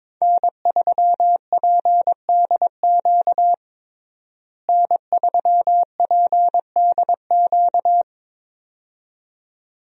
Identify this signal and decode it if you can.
Morse code, heard as a keyed tone, about 22 wpm, 700 Hz. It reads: N3PDQ N3PDQ
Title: USA Call Signs - 5 Characters - Repeated 2x - 22wpm